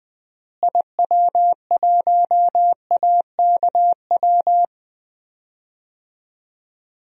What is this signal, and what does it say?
Morse code, heard as a keyed tone, about 21 wpm, 700 Hz. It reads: IW1AKW
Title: DXCC-100 Call Signs - 6 Characters - Repeated 3x - 20wpm